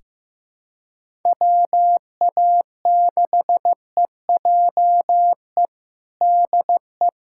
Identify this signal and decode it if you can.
Morse code, heard as a keyed tone, about 15 wpm, 700 Hz. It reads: WA6EJE DE
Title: USA Callsign DE Callsign - 6 Characters - 15wpm